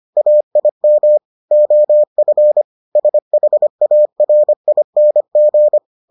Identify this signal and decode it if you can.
Morse code, heard as a keyed tone, about 25 wpm, 600 Hz. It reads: AIM OF SHARING